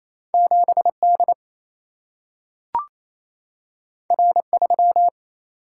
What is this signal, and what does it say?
Morse code, heard as a keyed tone, about 28 wpm, 700 Hz. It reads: 7 D R 3
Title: Single-Letter Single-Number 28wpm